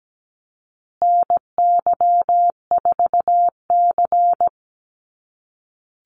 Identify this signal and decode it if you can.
Morse code, heard as a keyed tone, about 17 wpm, 700 Hz. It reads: NY4C
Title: USA Call Signs Repeated 3x 17wpm